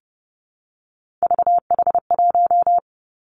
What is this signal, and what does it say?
Morse code, heard as a keyed tone, about 30 wpm, 700 Hz. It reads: VH1
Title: Three Challenging Letter-Number Combinations 30wpm